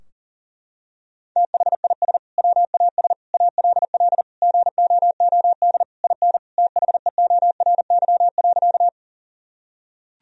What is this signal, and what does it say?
Morse code, heard as a keyed tone, about 40 wpm, 700 Hz. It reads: THIS WAS ALL GOOD IN THEORY.
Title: Sentences from Top 1000 Words - Review Part 3 - 40wpm